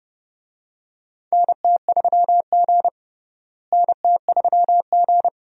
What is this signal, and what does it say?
Morse code, heard as a keyed tone, about 30 wpm, 700 Hz. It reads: NT3G NT3G
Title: USA Call Signs - 4 Characters - Repeated 2x - 30wpm